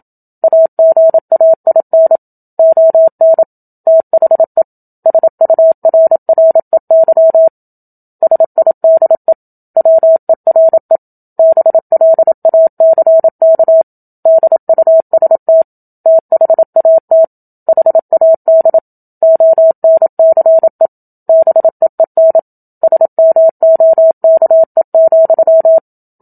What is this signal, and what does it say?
Morse code, heard as a keyed tone, about 27 wpm, 650 Hz. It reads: AGAIN ON THE SURREY SIDE WERE BLACK DUST THAT HAD ONCE BEEN SMOKE,
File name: Book Two / 01-Chapter 1 - 27 WPM